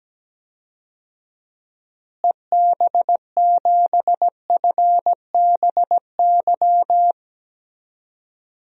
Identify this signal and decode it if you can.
Morse code, heard as a keyed tone, about 17 wpm, 700 Hz. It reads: EB7FBY
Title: DXCC-100 Call Signs - 6 Characters - Repeated 3x - 17wpm